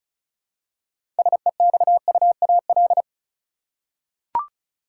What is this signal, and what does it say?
Morse code, heard as a keyed tone, about 36 wpm, 700 Hz. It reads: SEXUAL
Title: Six Letter Words - 35wpm